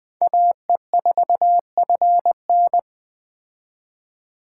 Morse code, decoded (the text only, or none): AE4FN